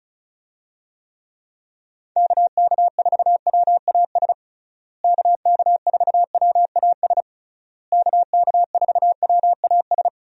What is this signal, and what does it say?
Morse code, heard as a keyed tone, about 36 wpm, 700 Hz. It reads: KK4WAS KK4WAS KK4WAS
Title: USA Call Signs - 6 Characters - Repeated 3x - 35wpm